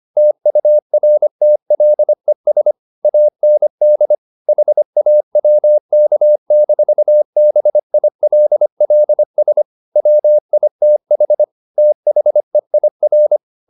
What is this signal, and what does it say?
Morse code, heard as a keyed tone, about 25 wpm, 600 Hz. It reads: TURTLES AND HAWK-BILLS WITH THEIR
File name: the old man and the sea part 1 25wpm